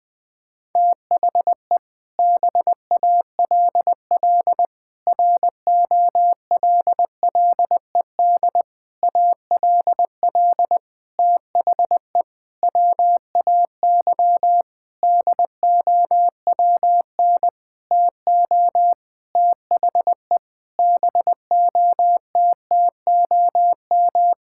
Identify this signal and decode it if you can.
Morse code, heard as a keyed tone, about 20 wpm, 700 Hz. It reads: THE BALL ROLLED ALL THE WAY DOWN TO THE BOTTOM
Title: Sentences with 6-Word Phrases - Spoken Beginning - 20wpm